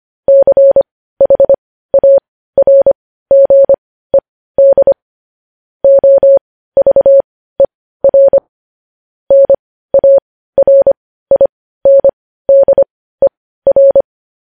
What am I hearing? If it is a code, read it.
Morse code, heard as a keyed tone, about 25 wpm, spaced as for 15 wpm, 550 Hz. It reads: CHARGED OVER NARINDER